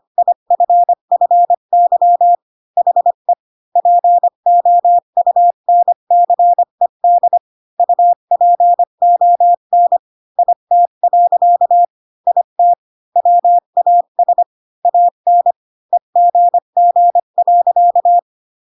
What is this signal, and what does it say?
Morse code, heard as a keyed tone, about 25 wpm, 700 Hz. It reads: IFFY HE POUNCED UPON IT. IT WAS AN EGG.
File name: Pinocchio / Pinocchio ch5 20 25 ik1hge